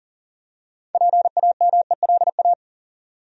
Morse code, heard as a keyed tone, about 40 wpm, 700 Hz. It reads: PAMELA
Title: USA Common Names - 40wpm